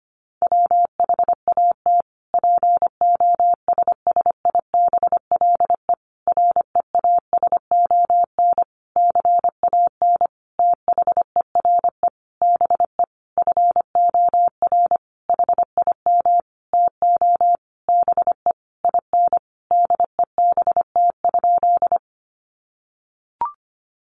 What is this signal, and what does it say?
Morse code, heard as a keyed tone, about 25 wpm, 700 Hz. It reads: WHAT POSSIBLE REASON CAN THERE BE FOR HIM TO BE IN DEBT?
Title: Sentences from Top 1300 Words 25wpm